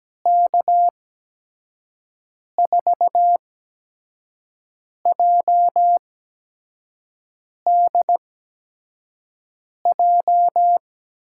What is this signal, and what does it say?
Morse code, heard as a keyed tone, about 17 wpm, spaced as for 5 wpm, 700 Hz. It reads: K4JDJ
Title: USA Call Signs - 8x Character Spacing - 17wpm